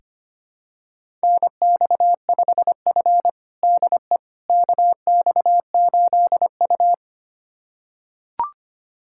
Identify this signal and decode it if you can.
Morse code, heard as a keyed tone, about 25 wpm, 700 Hz. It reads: NX5F DE KX8U
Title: USA Callsign DE Callsign - 4 Characters - 25wpm